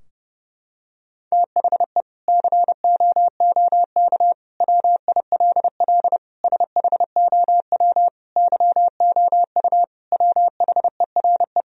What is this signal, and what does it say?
Morse code, heard as a keyed tone, about 32 wpm, 700 Hz. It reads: THE COOK WILL SHOW YOU WHERE
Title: Sentences from Top 1000 Words - Review Part 4 - 30wpm